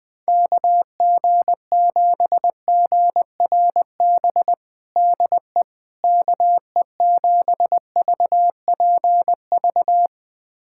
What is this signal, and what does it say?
Morse code, heard as a keyed tone, about 20 wpm, 700 Hz. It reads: KG7GRB DE KE7VPV
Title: USA Callsign DE Callsign - 6 Characters - 20wpm